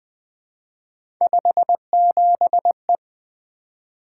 Morse code, heard as a keyed tone, about 20 wpm, 700 Hz. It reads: 57E
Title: Three Letter-Number Combinations - 20wpm